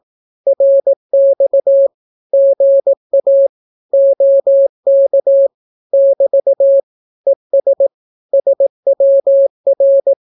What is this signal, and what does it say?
Morse code, heard as a keyed tone, about 18 wpm, 550 Hz. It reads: RX GA OK <BT> ES SWR
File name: naukapodstaw / skroty / skroty